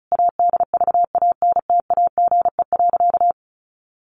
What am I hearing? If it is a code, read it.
Morse code, heard as a keyed tone, about 34 wpm, 700 Hz. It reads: ADVANTAGE.
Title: Sentences from Top 1400 Words 35wpm